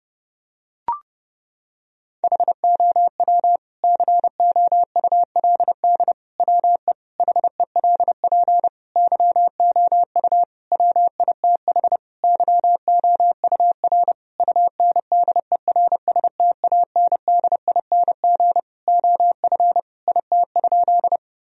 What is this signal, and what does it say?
Morse code, heard as a keyed tone, about 32 wpm, 700 Hz. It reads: HOW COULD WE HELP YOU WITH YOUR UNDERSTANDING OF IT?
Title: Sentences from Top 1000 Words - Review Part 3 - 30wpm